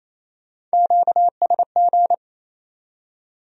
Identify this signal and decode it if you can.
Morse code, heard as a keyed tone, about 35 wpm, 700 Hz. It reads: QSG